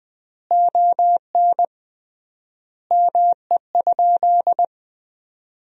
Morse code, heard as a keyed tone, about 20 wpm, 700 Hz. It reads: ON ME?
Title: 3x Word-Spacing - Sentences from Top 500 Words - 20wpm